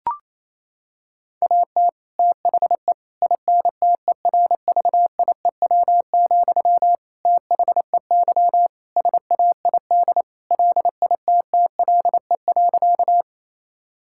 Morse code, heard as a keyed tone, about 27 wpm, 700 Hz. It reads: AT THE INTERVIEW, THEY SAID LITTLE.
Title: Sentences from Top 1600 Words 28wpm